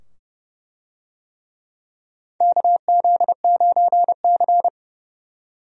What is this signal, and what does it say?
Morse code, heard as a keyed tone, about 29 wpm, 700 Hz. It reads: KZ9C
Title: USA Call Signs - 4 Characters - 30wpm